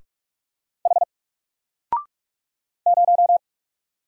Morse code, heard as a keyed tone, about 44 wpm, 700 Hz. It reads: H 0
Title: Single Letter-Number Alt 45wpm